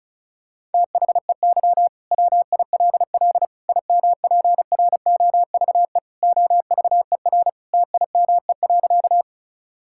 Morse code, heard as a keyed tone, about 34 wpm, 700 Hz. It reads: THEY WILL IMPROVE OVER TIME.
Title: Sentences from Top 1000 Words - Review Part 5 - 35wpm